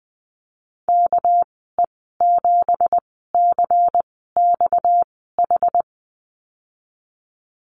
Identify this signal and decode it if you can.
Morse code, heard as a keyed tone, about 20 wpm, spaced as for 14 wpm, 700 Hz. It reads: KE7CXH
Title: 238 Lesson 38g - Letter q - Call Signs 20wpm